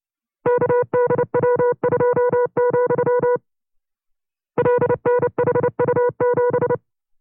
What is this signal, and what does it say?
Morse code, heard as a keyed tone, about 29 wpm, 490 Hz. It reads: KDW2, LNHU7
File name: CW-practice/25WPM